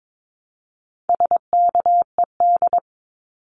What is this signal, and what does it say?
Morse code, heard as a keyed tone, about 22 wpm, 700 Hz. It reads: SKED
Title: Abbreviations Repeated 3x - 22wpm